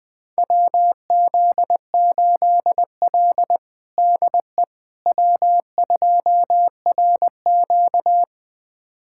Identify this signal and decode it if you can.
Morse code, heard as a keyed tone, about 20 wpm, 700 Hz. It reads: WZ8L DE W2RQ
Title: USA Callsign DE Callsign - 4 Characters - 20wpm